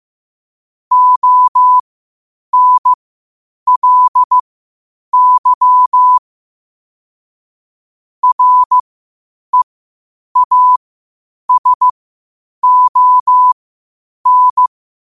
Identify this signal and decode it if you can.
Morse code, heard as a keyed tone, about 15 wpm, spaced as for 8 wpm, 1000 Hz. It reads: ONLY REASON
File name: qotdfiles / 8wpm